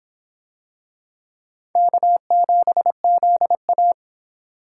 Morse code, heard as a keyed tone, about 25 wpm, 700 Hz. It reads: K7ZA